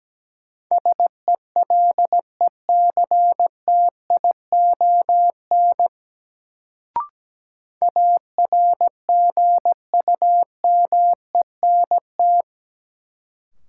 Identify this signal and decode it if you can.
Morse code, heard as a keyed tone, about 17 wpm, 700 Hz. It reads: SELECTION ARGUMENT
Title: Sentences from Top 1400 Words - New Word Drill Alt - 17wpm